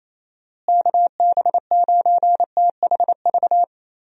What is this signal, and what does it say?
Morse code, heard as a keyed tone, about 28 wpm, 700 Hz. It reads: KB9THV
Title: USA Call Signs - 6 Characters - Repeated 2x - 28wpm